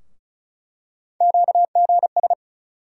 Morse code, heard as a keyed tone, about 36 wpm, 700 Hz. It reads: QGS
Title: USA Call Sign Postfixes - 3 Letters 35wpm